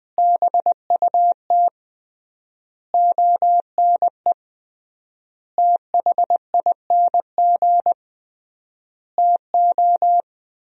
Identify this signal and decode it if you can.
Morse code, heard as a keyed tone, about 21 wpm, 700 Hz. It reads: BUT ONE THING TO
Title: 3x Word-Spacing - Sentences from Top 100 Words - 20wpm